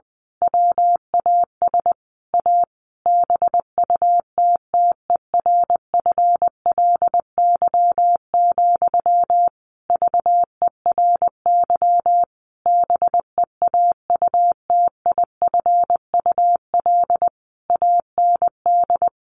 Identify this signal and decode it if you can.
Morse code, heard as a keyed tone, about 20 wpm, 700 Hz. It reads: WAS A BUTTERFLY, VERY BEAUTIFUL AND
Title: A Sound Of Thunder 20wpm